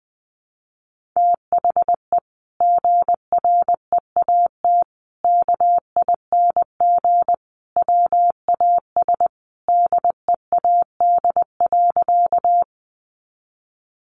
Morse code, heard as a keyed tone, about 20 wpm, 700 Hz. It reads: THE GREAT KING WAS DEAD.
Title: Sentences from Top 1500 Words 20wpm